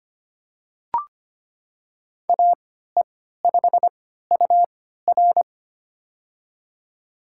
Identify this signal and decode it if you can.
Morse code, heard as a keyed tone, about 25 wpm, spaced as for 14 wpm, 700 Hz. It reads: AE5UR
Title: USA Call Signs - 5 Characters - 3x Character Spacing - 25wpm